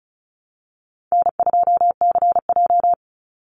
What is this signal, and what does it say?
Morse code, heard as a keyed tone, about 34 wpm, 700 Hz. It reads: N2CJ
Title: Call Signs - 35wpm